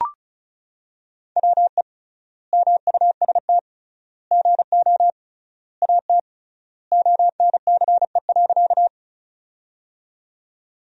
Morse code, heard as a keyed tone, about 36 wpm, 700 Hz. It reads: WE MUST GO AT ONCE.
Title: 3x Word-Spacing - Sentences from Top 300 Words - 35wpm